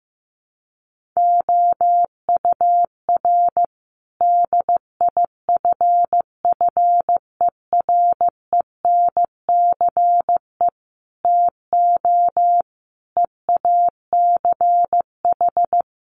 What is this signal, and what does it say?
Morse code, heard as a keyed tone, about 14 wpm, 700 Hz. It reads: OUR DIFFERENCE TO EACH
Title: Sentences from Top 1500 Words 15wpm